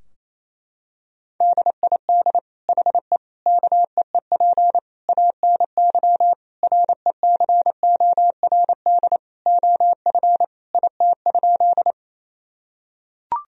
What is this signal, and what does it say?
Morse code, heard as a keyed tone, about 28 wpm, 700 Hz. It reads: DID HE KEEP ANY RECORD OF IT?
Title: Sentences from Top 1000 Words - Review Part 3 - 28wpm